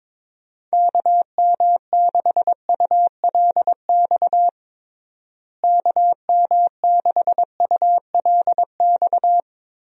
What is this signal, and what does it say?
Morse code, heard as a keyed tone, about 22 wpm, 700 Hz. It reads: KM6ULX KM6ULX
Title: USA Call Signs - 6 Characters - Repeated 2x - 22wpm